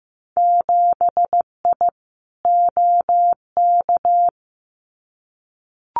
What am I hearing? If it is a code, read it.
Morse code, heard as a keyed tone, about 15 wpm, 700 Hz. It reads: 7I OK
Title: Winter Field Day Exchanges Encore 15wpm